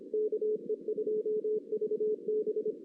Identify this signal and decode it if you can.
Morse code, heard as a keyed tone, about 25 wpm, 440 Hz. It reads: KE2VB
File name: 2016 JAN CW NAQP